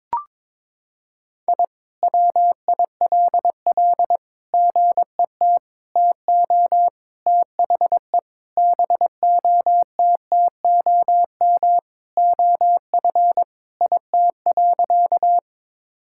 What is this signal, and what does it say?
Morse code, heard as a keyed tone, about 21 wpm, 700 Hz. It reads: I WILL GET TO THE BOTTOM OF IT.
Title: Sentences from Top 1000 Words - Review Part 6 - 22wpm